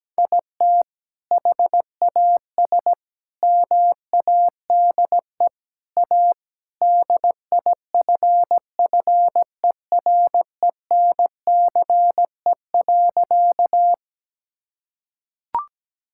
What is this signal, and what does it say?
Morse code, heard as a keyed tone, about 17 wpm, 700 Hz. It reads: IT HAS MADE A DIFFERENCE.
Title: Sentences from Top 1000 Words - Review Part 5 - 17wpm